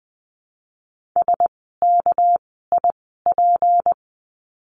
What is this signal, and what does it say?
Morse code, heard as a keyed tone, about 20 wpm, spaced as for 14 wpm, 700 Hz. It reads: SKIP
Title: 199 Lesson 32f - Letter k - Words 20wpm